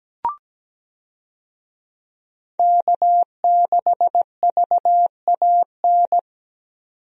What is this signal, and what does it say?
Morse code, heard as a keyed tone, about 17 wpm, 700 Hz. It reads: K6VAN
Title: USA Call Signs - 5 Characters - 17wpm